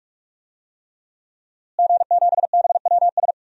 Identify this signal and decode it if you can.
Morse code, heard as a keyed tone, about 46 wpm, 700 Hz. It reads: G7BWS
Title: DXCC-100 Call Signs - 5 Characters - Repeated 3x - 45wpm